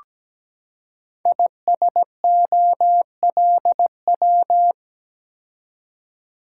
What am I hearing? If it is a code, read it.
Morse code, heard as a keyed tone, about 17 wpm, 700 Hz. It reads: ISOLW
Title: Code Groups - 5 Letter:Numbers - 17wpm